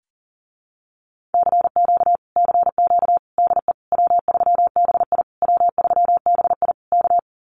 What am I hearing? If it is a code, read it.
Morse code, heard as a keyed tone, about 37 wpm, 700 Hz. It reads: CQ CQ DE W3BI W3BI K
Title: Common QSO Phrases Encore 40wpm